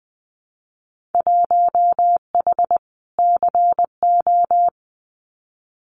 Morse code, heard as a keyed tone, about 20 wpm, 700 Hz. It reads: 1H CO
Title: Winter Field Day Exchanges 20wpm